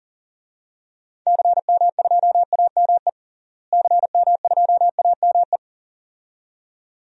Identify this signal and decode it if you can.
Morse code, heard as a keyed tone, about 40 wpm, 700 Hz. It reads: CM2AME CM2AME
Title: DXCC-100 Call Signs - 6 Characters - Repeated 2x - 40wpm